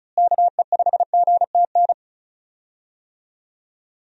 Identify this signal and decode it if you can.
Morse code, heard as a keyed tone, about 36 wpm, 700 Hz. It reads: KE5GTN
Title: USA Call Signs - 6 Characters - Repeated 2x - 35wpm